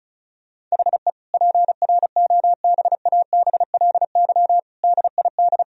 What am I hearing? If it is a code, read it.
Morse code, heard as a keyed tone, about 34 wpm, 700 Hz. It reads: HE PROBABLY DID
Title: Sentences from Top 1000 Words - Review Part 4 - 35wpm